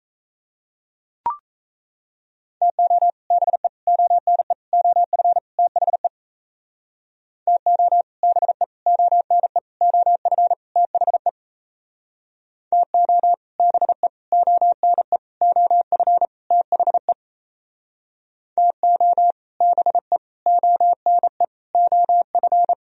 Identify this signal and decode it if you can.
Morse code, heard as a keyed tone, about 36 wpm, 700 Hz. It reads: TO BE ONE OF THE TO BE ONE OF THE TO BE ONE OF THE TO BE ONE OF